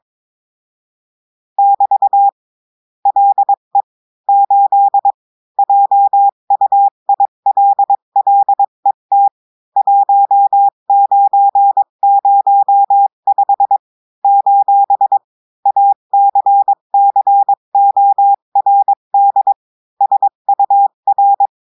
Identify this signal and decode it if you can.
Morse code, heard as a keyed tone, about 22 wpm, 800 Hz. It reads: <BT> LE 8 JUILLET 1905 : ACCORD SUR
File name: cours-du-lundi files / cours-1580-texte-clair-20-mpm